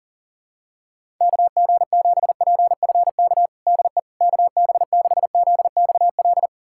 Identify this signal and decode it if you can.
Morse code, heard as a keyed tone, about 37 wpm, 700 Hz. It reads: KG7PFK DE KB6ZXL
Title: USA Callsign DE Callsign - 6 Characters - 40wpm